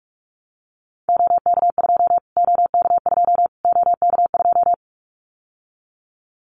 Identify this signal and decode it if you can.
Morse code, heard as a keyed tone, about 44 wpm, 700 Hz. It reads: OK2 OK2 OK2
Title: Callsign Prefixes for Top 25 DXCC - Repeated 3x - 45wpm